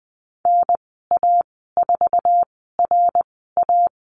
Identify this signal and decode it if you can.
Morse code, heard as a keyed tone, about 20 wpm, spaced as for 14 wpm, 700 Hz. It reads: NA4RA
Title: 062 Lesson 11g - Letter h - Call Signs 20wpm